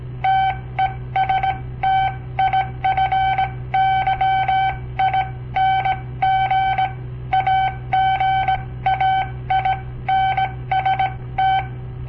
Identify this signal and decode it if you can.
Morse code, heard as a keyed tone, about 18 wpm, spaced as for 15 wpm, 770 Hz. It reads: TESTIFYING AGAINST
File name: papertapes / local / Code Practice-01